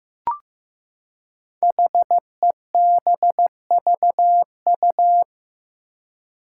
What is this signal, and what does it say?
Morse code, heard as a keyed tone, about 15 wpm, 700 Hz. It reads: HEBVU
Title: Five Challenging Letter-Number Combinations 15wpm